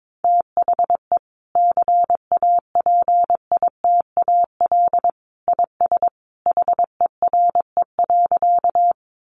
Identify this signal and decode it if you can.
Morse code, heard as a keyed tone, about 22 wpm, 700 Hz. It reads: THE CAPITAL IS HERE.
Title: Sentences from Top 1000 Words 22wpm